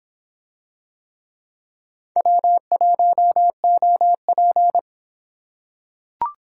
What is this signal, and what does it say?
Morse code, heard as a keyed tone, about 26 wpm, 700 Hz. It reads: W1OP